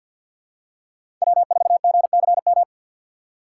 Morse code, heard as a keyed tone, about 53 wpm, 700 Hz. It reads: W4GCM
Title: USA Call Signs - 5 Characters - 4x Character Spacing - 50wpm